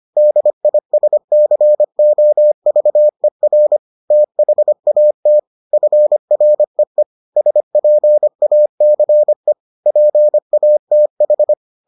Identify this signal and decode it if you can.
Morse code, heard as a keyed tone, about 25 wpm, 600 Hz. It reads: DISCOVER THAT FREE SPACE PATH